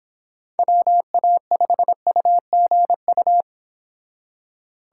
Morse code, heard as a keyed tone, about 26 wpm, 700 Hz. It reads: WA5UGU